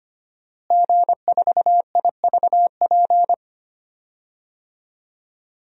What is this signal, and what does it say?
Morse code, heard as a keyed tone, about 25 wpm, 700 Hz. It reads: G4IVP
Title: DXCC-100 Call Signs - 5 Characters - Repeated 3x - 25wpm